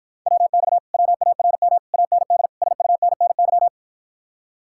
Morse code, heard as a keyed tone, about 53 wpm, 700 Hz. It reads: WX WARM AND SUNNY